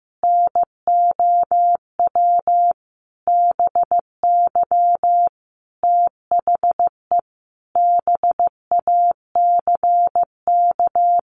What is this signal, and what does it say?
Morse code, heard as a keyed tone, about 15 wpm, 700 Hz. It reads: NOW BY THE BACK
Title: Sentences from Top 300 Words 15wpm